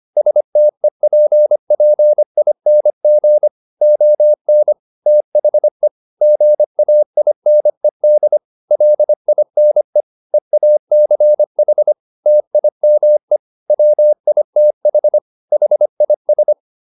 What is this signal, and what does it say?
Morse code, heard as a keyed tone, about 25 wpm, 600 Hz. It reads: STEPPING ON THE GAINED LINE EACH TIME WITH HIS